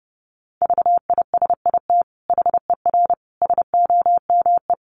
Morse code, heard as a keyed tone, about 30 wpm, 700 Hz. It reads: VISIT HER SOME
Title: Sentences from Top 900 Words 30wpm